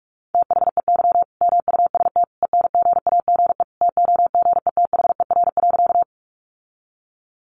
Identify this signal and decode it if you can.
Morse code, heard as a keyed tone, about 44 wpm, 700 Hz. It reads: THEY MUST ENGAGE TOGETHER.
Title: Sentences from Top 1500 Words 45wpm